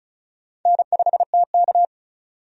Morse code, heard as a keyed tone, about 36 wpm, 690 Hz. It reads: N5TK
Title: USA Call Signs - 4 Characters - 35wpm